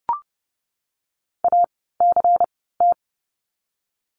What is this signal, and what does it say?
Morse code, heard as a keyed tone, about 30 wpm, spaced as for 17 wpm, 700 Hz. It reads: ACT
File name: Lesson 16f - Letter c - Words 30wpm